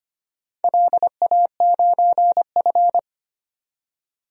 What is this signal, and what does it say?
Morse code, heard as a keyed tone, about 25 wpm, 700 Hz. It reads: LA9F
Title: DXCC-100 Call Signs - 4 Characters - 25wpm